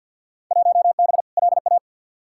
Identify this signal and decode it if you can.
Morse code, heard as a keyed tone, about 52 wpm, 700 Hz. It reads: 1B LA
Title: Field Day Exchanges Encore - 50wpm